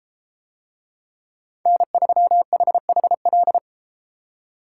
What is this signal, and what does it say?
Morse code, heard as a keyed tone, about 31 wpm, 700 Hz. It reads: N3HHL